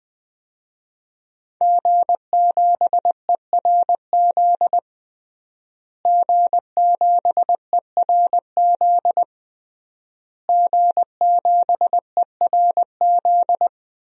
Morse code, heard as a keyed tone, about 20 wpm, 700 Hz. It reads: G7ERZ G7ERZ G7ERZ
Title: DXCC-100 Call Signs - 5 Characters - Repeated 3x - 20wpm